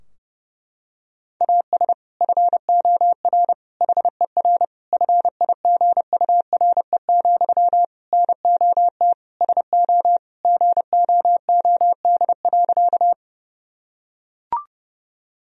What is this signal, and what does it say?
Morse code, heard as a keyed tone, about 30 wpm, 700 Hz. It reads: AS FOR HER FIGURE, NOT SO GOOD.
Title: Sentences from Top 1000 Words - Review Part 2 - 30wpm